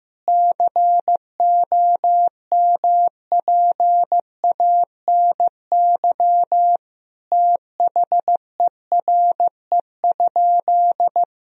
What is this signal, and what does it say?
Morse code, heard as a keyed tone, about 15 wpm, 700 Hz. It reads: COMPANY THERE?
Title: Sentences from Top 1000 Words - Review Part 3 - 15wpm